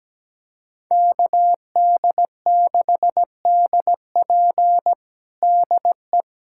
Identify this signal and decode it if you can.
Morse code, heard as a keyed tone, about 17 wpm, 700 Hz. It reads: KD6DP DE
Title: USA Callsign DE Callsign - 5 Characters - 17wpm